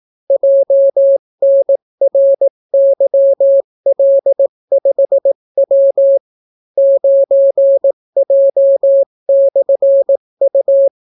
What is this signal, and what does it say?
Morse code, heard as a keyed tone, about 18 wpm, 550 Hz. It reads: JNRYL5W 9J/U